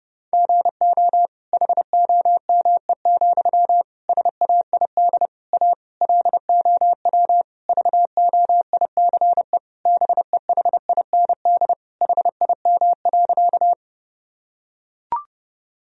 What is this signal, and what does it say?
Morse code, heard as a keyed tone, about 30 wpm, 700 Hz. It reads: GO HOME, SAID A LOW VOICE BEHIND HIM.
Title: Sentences from Top 1000 Words - Review Part 1 - 30wpm